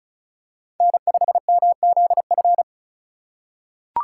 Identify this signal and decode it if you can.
Morse code, heard as a keyed tone, about 34 wpm, 700 Hz. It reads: N5MZF
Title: USA Call Signs - 5 Characters - 4x Character Spacing - 35wpm